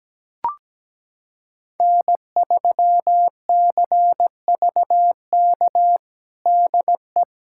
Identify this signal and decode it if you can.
Morse code, heard as a keyed tone, about 17 wpm, 700 Hz. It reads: N3CVK DE
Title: USA Callsign DE Callsign - 5 Characters - 17wpm